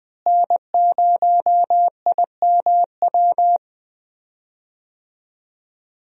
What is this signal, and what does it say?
Morse code, heard as a keyed tone, about 20 wpm, 700 Hz. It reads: N0IMW
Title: USA Call Signs - 5 Characters - Repeated 2x - 20wpm